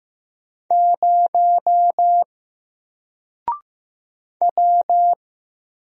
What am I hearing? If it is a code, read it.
Morse code, heard as a keyed tone, about 15 wpm, 700 Hz. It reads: 0 W
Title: Single Character (Letter:Number:Punctuation:Prosign) Alt 15wpm